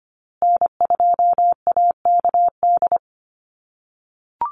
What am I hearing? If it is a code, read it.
Morse code, heard as a keyed tone, about 25 wpm, 700 Hz. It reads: N2AKD
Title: USA Call Signs Repeated 3x 25wpm